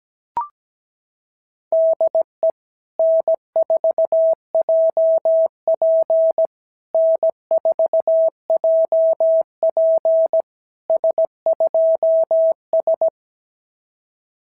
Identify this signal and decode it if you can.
Morse code, heard as a keyed tone, about 17 wpm, 650 Hz. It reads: DE N4JP N4JP S2S
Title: SOTA QSO Practice 17wpm